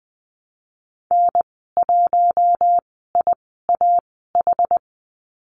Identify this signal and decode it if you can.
Morse code, heard as a keyed tone, about 20 wpm, spaced as for 15 wpm, 700 Hz. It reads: N1IAH
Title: 062 Lesson 11g - Letter h - Call Signs 20wpm